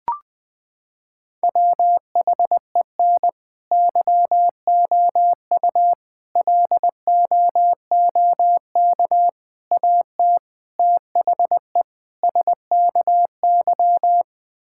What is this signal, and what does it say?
Morse code, heard as a keyed tone, about 20 wpm, 700 Hz. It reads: WHEN YOU LOOK AT THE SKY
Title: Sentences with 5-Word Phrases - 20wpm